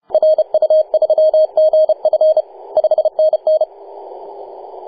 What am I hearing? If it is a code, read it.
Morse code, heard as a keyed tone, about 32 wpm, 620 Hz. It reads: RU3GF 5NN
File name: posts / 2015-nov-18-0552z-vk9wa-24-894mhz-cw